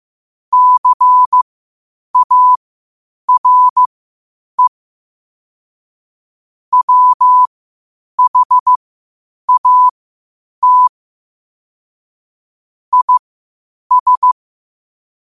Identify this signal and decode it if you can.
Morse code, heard as a keyed tone, about 15 wpm, spaced as for 8 wpm, 1000 Hz. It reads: CARE WHAT IS
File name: qotdfiles / 8wpm